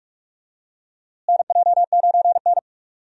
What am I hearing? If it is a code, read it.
Morse code, heard as a keyed tone, about 46 wpm, 700 Hz. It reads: NJ9N
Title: USA Call Signs - 4 Characters - Repeated 2x - 45wpm